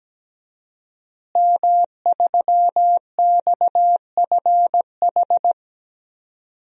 Morse code, heard as a keyed tone, about 17 wpm, 700 Hz. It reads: M3XFH
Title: DXCC-100 Call Signs - 5 Characters - Repeated 2x - 17wpm